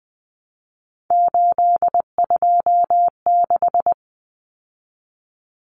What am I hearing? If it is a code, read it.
Morse code, heard as a keyed tone, about 20 wpm, 700 Hz. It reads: 826
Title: Common Cut Numbers - 3-Digit 20wpm